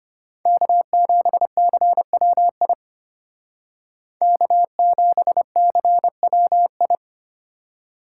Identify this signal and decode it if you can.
Morse code, heard as a keyed tone, about 32 wpm, 700 Hz. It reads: K7CWI K7CWI